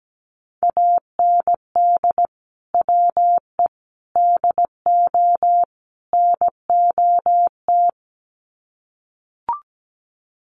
Morse code, heard as a keyed tone, about 17 wpm, 700 Hz. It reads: AND WE DO NOT
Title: Sets of 4 Words 17wpm